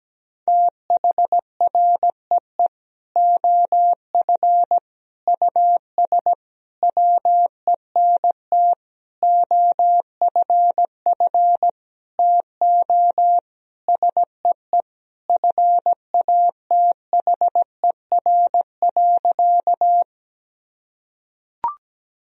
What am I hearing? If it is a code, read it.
Morse code, heard as a keyed tone, about 17 wpm, 700 Hz. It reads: THREE OF US WENT OFF TO SEE FATHER.
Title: Sentences from Top 1000 Words - Review Part 1 - 17wpm